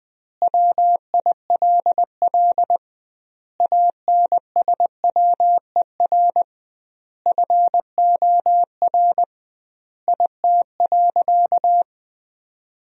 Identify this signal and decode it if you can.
Morse code, heard as a keyed tone, about 21 wpm, 700 Hz. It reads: WILL ANSWER FOR IT.
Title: 2x Word-Spacing - Sentences from Top 300 Words - 20wpm